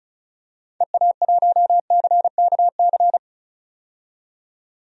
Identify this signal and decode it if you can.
Morse code, heard as a keyed tone, about 36 wpm, 700 Hz. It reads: EA1CKC
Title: DXCC-100 Call Signs - 6 Characters - 35wpm